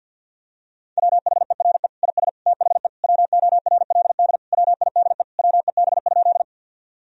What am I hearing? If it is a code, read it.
Morse code, heard as a keyed tone, about 50 wpm, 700 Hz. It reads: WHERE IS THE WORLD WIDE WEB?
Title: Sentences from Top 1000 Words - Review Part 4 - 50wpm